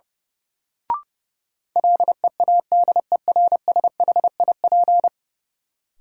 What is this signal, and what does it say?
Morse code, heard as a keyed tone, about 30 wpm, 700 Hz. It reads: LEADERSHIP
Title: Sentences from Top 1600 Words - New Word Drill Alt - 30wpm